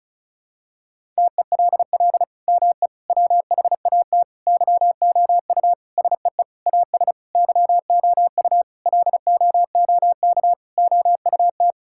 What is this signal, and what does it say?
Morse code, heard as a keyed tone, about 34 wpm, 700 Hz. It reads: TELL ME WHAT YOU SEE AS YOU LOOK OUT
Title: Sentences from Top 1600 Words 35wpm